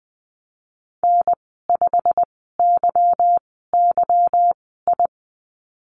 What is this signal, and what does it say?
Morse code, heard as a keyed tone, about 20 wpm, spaced as for 15 wpm, 700 Hz. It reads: N5YYI
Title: 154 Lesson 25g - Letter p - Call Signs 20wpm